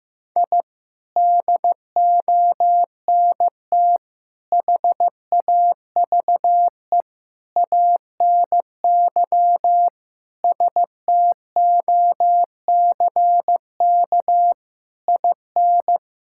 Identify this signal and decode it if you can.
Morse code, heard as a keyed tone, about 15 wpm, 700 Hz. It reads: I DONT HAVE ANY STOCK IN
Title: Sentences from Top 1000 Words - Review Part 5 - 15wpm